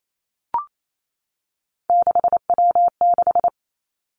Morse code, heard as a keyed tone, about 28 wpm, 700 Hz. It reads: 6W6
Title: Three Challenging Letter-Number Combinations 28wpm